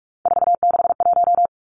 Morse code, heard as a keyed tone, about 44 wpm, 700 Hz. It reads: 461
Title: Numbers - 3-Digit 45wpm